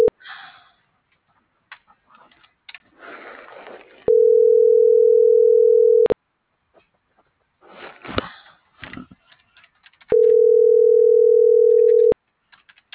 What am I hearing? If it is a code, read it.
Ringback tone (North America), 440 + 480 Hz.